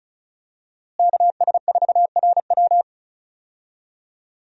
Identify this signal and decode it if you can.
Morse code, heard as a keyed tone, about 36 wpm, 700 Hz. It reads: KS4RW
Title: USA Call Signs - 5 Characters - Repeated 3x - 35wpm